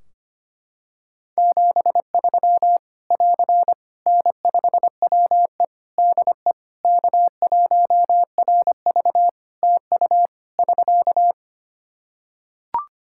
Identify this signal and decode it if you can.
Morse code, heard as a keyed tone, about 25 wpm, 700 Hz. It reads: 73 <AR> N5WE DE K1RV TU <SK>
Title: Practice QSO - Exchange 4 - Ending - 25wpm